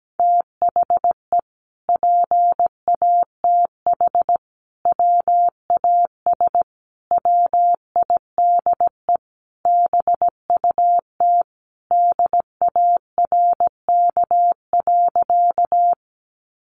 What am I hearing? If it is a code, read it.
Morse code, heard as a keyed tone, about 17 wpm, 700 Hz. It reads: THE PATH WAS WIDE BUT DARK.
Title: Sentences from Top 1400 Words 17wpm